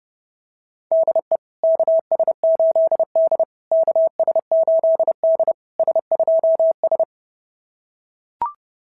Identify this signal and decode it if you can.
Morse code, heard as a keyed tone, about 30 wpm, 690 Hz. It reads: DE KS8D KS8D S2S
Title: SOTA QSO Practice 30wpm